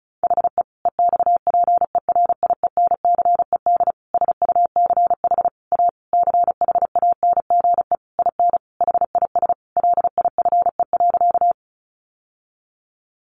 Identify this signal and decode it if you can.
Morse code, heard as a keyed tone, about 34 wpm, 700 Hz. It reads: HE EXPERIENCED SUCH A CHANGE IN HIS LIFE.
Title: Sentences from Top 1100 Words 35wpm